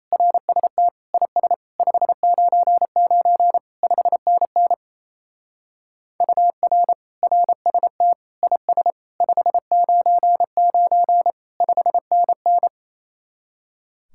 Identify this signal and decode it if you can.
Morse code, heard as a keyed tone, about 34 wpm, 700 Hz. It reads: RST IS 599 5NN UR RST IS 599 5NN
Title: Speed-Racing Common QSO Phrases - 28wpm